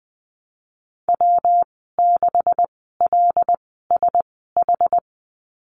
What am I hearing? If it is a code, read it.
Morse code, heard as a keyed tone, about 20 wpm, spaced as for 15 wpm, 700 Hz. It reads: W6LSH
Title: 140 Lesson 23g - Letter f - Call Signs 20wpm